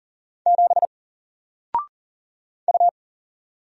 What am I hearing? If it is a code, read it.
Morse code, heard as a keyed tone, about 43 wpm, 700 Hz. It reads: 7 U
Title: Single Letter-Number Alt 40wpm